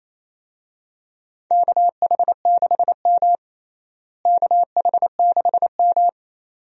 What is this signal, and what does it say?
Morse code, heard as a keyed tone, about 27 wpm, 700 Hz. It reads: KH6M KH6M
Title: USA Call Signs - 4 Characters - Repeated 2x - 28wpm